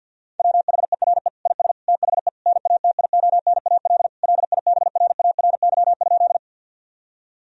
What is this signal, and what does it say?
Morse code, heard as a keyed tone, about 52 wpm, 700 Hz. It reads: WHERE IS THE NATIONAL LIBRARY?
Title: Sentences from Top 1000 Words - Review Part 4 - 50wpm